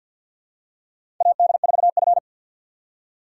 Morse code, heard as a keyed tone, about 50 wpm, 700 Hz. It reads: AD4F
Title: USA Call Signs - 2x Character Spacing - 50wpm